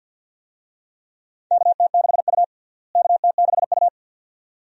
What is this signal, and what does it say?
Morse code, heard as a keyed tone, about 46 wpm, 700 Hz. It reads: KT6U KT6U
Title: USA Call Signs - 4 Characters - Repeated 2x - 50wpm